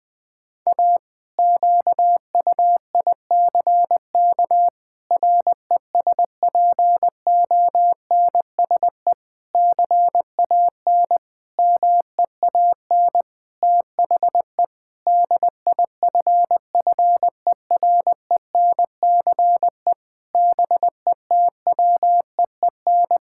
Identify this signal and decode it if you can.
Morse code, heard as a keyed tone, about 20 wpm, 700 Hz. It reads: A QUICK RESPONSE CAN MEAN THE DIFFERENCE BETWEEN
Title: Sentences with 8-Word Phrases - 20wpm